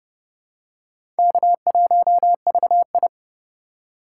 Morse code, heard as a keyed tone, about 30 wpm, 700 Hz. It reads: K1VI
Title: USA Call Signs - 4 Characters - 8x Character Spacing - 30wpm